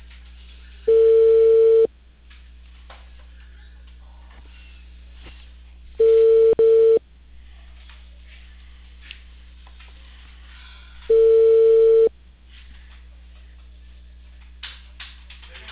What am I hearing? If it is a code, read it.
Ringback tone (China, 450 Hz), with speech.